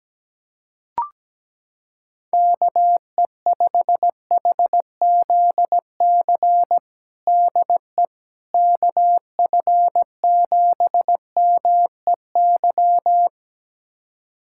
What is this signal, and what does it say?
Morse code, heard as a keyed tone, about 17 wpm, 700 Hz. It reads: KE5HZC DE KF7MEY
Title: USA Callsign DE Callsign - 6 Characters - 17wpm